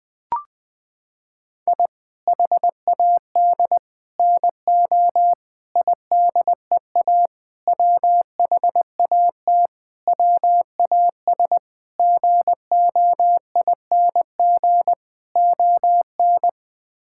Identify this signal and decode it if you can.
Morse code, heard as a keyed tone, about 21 wpm, 700 Hz. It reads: I HAD NO IDEA WHAT WAS GOING ON
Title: Sets of 8 Words - 20wpm